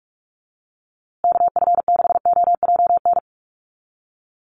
Morse code, heard as a keyed tone, about 44 wpm, 700 Hz. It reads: KF6OJN
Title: USA Call Signs Repeated 2x 45wpm